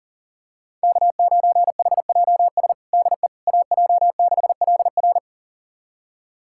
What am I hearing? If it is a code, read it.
Morse code, heard as a keyed tone, about 37 wpm, 690 Hz. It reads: K9HJS DE AJ6LR
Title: USA Callsign DE Callsign - 5 Characters - 40wpm